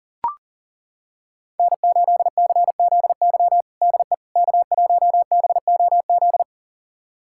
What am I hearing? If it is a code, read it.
Morse code, heard as a keyed tone, about 40 wpm, 700 Hz. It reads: N8CZY DE K1BOZ
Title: USA Callsign DE Callsign - 5 Characters - 40wpm